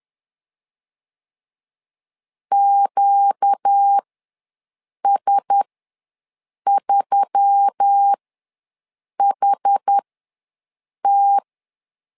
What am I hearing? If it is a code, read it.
Morse code, heard as a keyed tone, about 10 wpm, spaced as for 6 wpm, 800 Hz. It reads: QS3HT
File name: LEZIONI / 4 Q1S3 ws 10wpm 2437